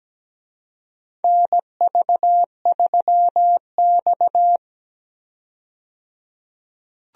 Morse code, heard as a keyed tone, about 17 wpm, 700 Hz. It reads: NV3X
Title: USA Call Signs - 4 Characters - 17wpm